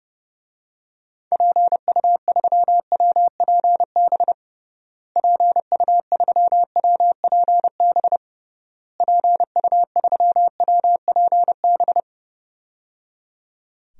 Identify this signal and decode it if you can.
Morse code, heard as a keyed tone, about 29 wpm, 700 Hz. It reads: PU3WPB PU3WPB PU3WPB
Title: DXCC-100 Call Signs - 6 Characters - Repeated 3x - 30wpm